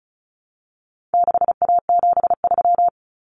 Morse code, heard as a keyed tone, about 34 wpm, 700 Hz. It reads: 6A73
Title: Common Cut Numbers - 4-Digit 35wpm